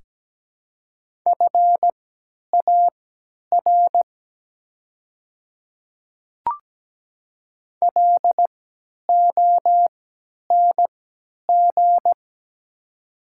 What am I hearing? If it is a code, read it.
Morse code, heard as a keyed tone, about 17 wpm, spaced as for 10 wpm, 700 Hz. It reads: FAR LONG
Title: 3x Character Spacing - Top 300 Words - 17wpm